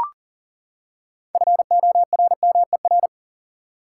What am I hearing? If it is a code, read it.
Morse code, heard as a keyed tone, about 40 wpm, 700 Hz. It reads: FORMER
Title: Six Letter Words - 40wpm